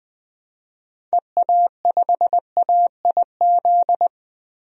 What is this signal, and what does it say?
Morse code, heard as a keyed tone, about 21 wpm, 700 Hz. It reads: EA5AIZ
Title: DXCC-100 Call Signs - 6 Characters - 20wpm